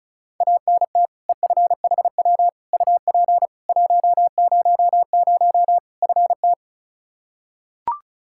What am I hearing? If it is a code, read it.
Morse code, heard as a keyed tone, about 36 wpm, 700 Hz. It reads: ANT EFHW UP 100 FT
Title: Practice QSO - Exchange 2 - Info - 35wpm